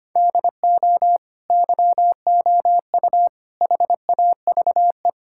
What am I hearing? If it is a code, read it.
Morse code, heard as a keyed tone, about 25 wpm, 700 Hz. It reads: DO YOU HAVE
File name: practice / 25wpm / Sentences with 9-Word Phrases - 25wpm